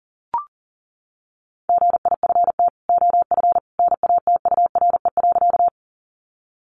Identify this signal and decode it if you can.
Morse code, heard as a keyed tone, about 37 wpm, 700 Hz. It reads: GIFT OF NATURE.
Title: Sentences from Top 900 Words 40wpm